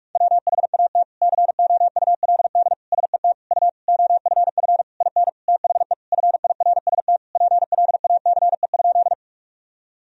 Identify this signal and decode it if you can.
Morse code, heard as a keyed tone, about 46 wpm, 700 Hz. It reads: WHAT COULD SET U OFF IN THE FIRST PLACE?
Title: Sentences from Top 200 Words and Abbreviations 45wpm